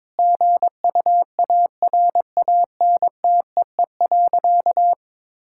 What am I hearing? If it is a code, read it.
Morse code, heard as a keyed tone, about 22 wpm, 700 Hz. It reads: GUARANTEE.
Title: Sentences from Top 1600 Words 22wpm